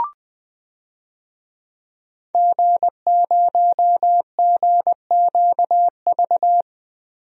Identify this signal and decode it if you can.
Morse code, heard as a keyed tone, about 20 wpm, 700 Hz. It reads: G0GQV
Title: DXCC-100 Call Signs - 5 Characters - 20wpm